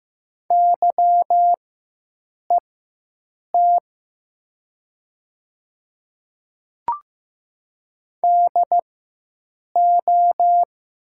Morse code, heard as a keyed tone, about 15 wpm, spaced as for 7 wpm, 700 Hz. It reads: YET DO
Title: 4x Character Spacing - Top 500 Words - 15wpm